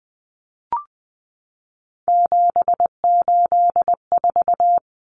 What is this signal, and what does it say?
Morse code, heard as a keyed tone, about 20 wpm, 700 Hz. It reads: 784
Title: Common Cut Numbers - 3-Digit 20wpm